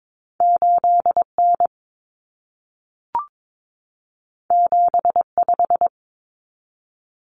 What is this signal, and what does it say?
Morse code, heard as a keyed tone, about 22 wpm, 700 Hz. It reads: 8N 75
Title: Common Cut Numbers - 2-Digit 22wpm